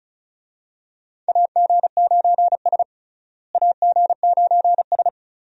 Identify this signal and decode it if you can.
Morse code, heard as a keyed tone, about 36 wpm, 700 Hz. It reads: AG9S AG9S
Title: USA Call Signs - 4 Characters - Repeated 2x - 35wpm